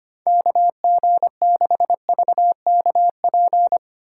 Morse code, heard as a keyed tone, about 25 wpm, 700 Hz. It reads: KG6VKP
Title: USA Call Signs - 6 Characters - 6x Character Spacing - 25wpm